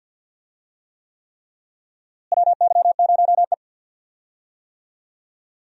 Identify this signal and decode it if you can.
Morse code, heard as a keyed tone, about 49 wpm, 700 Hz. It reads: WY0E
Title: USA Call Signs - 4 Characters - Repeated 3x - 50wpm